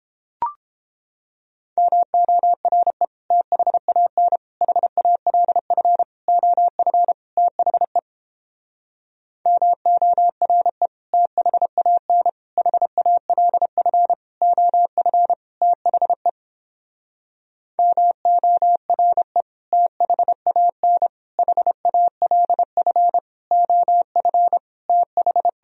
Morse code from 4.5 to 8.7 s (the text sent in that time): HALF OF THE